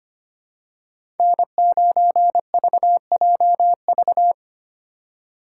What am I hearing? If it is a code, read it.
Morse code, heard as a keyed tone, about 25 wpm, 700 Hz. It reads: N9VJV
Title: USA Call Signs - 5 Characters - 3x Character Spacing - 25wpm